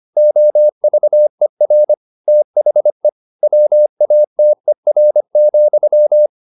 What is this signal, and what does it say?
Morse code, heard as a keyed tone, about 25 wpm, 600 Hz. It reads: OVER THE WATER,
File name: the old man and the sea part 1 25wpm